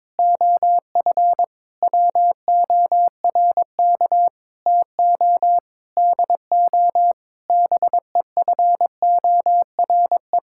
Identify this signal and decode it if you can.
Morse code, heard as a keyed tone, about 22 wpm, 700 Hz. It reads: OF WORK TO DO BEFORE
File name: practice / 22wpm / Sentences with 8-Word Phrases - 22wpm